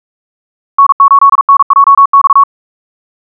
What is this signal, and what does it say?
Morse code, heard as a keyed tone, about 42 wpm, 1110 Hz. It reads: N8NJX
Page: SampleHST1.wav <-- 158cpm (=47wpm(paris)) 1111Hz sinusoidal waveform